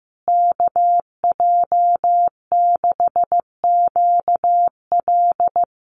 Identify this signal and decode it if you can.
Morse code, heard as a keyed tone, about 15 wpm, 700 Hz. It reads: KJ6QL
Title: USA Call Signs Repeated 3x 15wpm